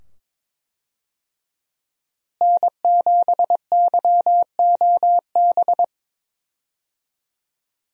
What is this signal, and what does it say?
Morse code, heard as a keyed tone, about 22 wpm, 700 Hz. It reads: N7YOB
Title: USA Call Signs - 5 Characters - Repeated 2x - 22wpm